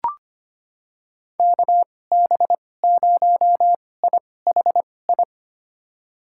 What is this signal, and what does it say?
Morse code, heard as a keyed tone, about 25 wpm, spaced as for 18 wpm, 700 Hz. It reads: KB0IHI
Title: USA Call Signs - 2x Character Spacing - 25wpm